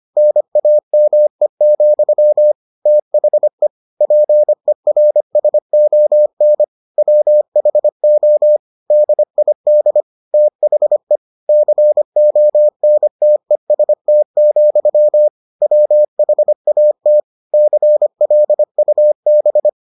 Morse code, heard as a keyed tone, about 25 wpm, 600 Hz. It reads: NAME, THE PERSON WHO DID THE CONTEST, WHAT CLUB